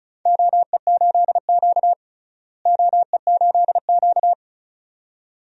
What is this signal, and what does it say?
Morse code, heard as a keyed tone, about 34 wpm, 700 Hz. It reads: OE8Q OE8Q
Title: DXCC-100 Call Signs - 4 Characters - Repeated 2x - 35wpm